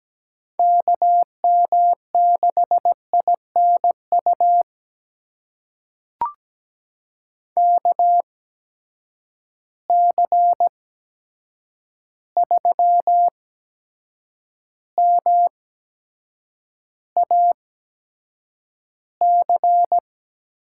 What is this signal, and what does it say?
Morse code, heard as a keyed tone, about 17 wpm, spaced as for 7 wpm, 700 Hz. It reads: KM6INU KC3MAC
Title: USA Call Signs - 6 Characters - 8x Character Spacing - 17wpm